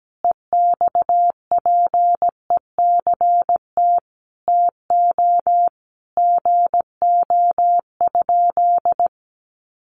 Morse code, heard as a keyed tone, about 17 wpm, 700 Hz. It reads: EXPECT TO GO?
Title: Sentences from Top 1100 Words 17wpm